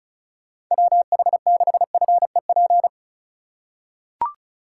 Morse code, heard as a keyed tone, about 34 wpm, 700 Hz. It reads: WH6FEP
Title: USA Call Signs - 6 Characters - 4x Character Spacing - 35wpm